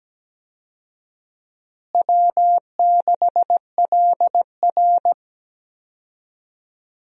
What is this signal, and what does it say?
Morse code, heard as a keyed tone, about 17 wpm, 700 Hz. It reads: W6LR
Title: USA Call Signs - 4 Characters - Repeated 3x - 17wpm